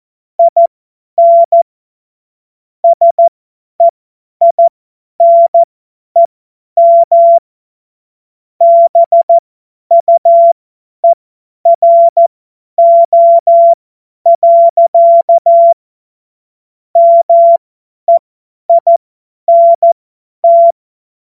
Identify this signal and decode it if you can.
Morse code, heard as a keyed tone, about 14 wpm, spaced as for 10 wpm, 680 Hz. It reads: IN SEINEM BUERO. MEINT
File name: Witz-14-Oktober-2025-50-F70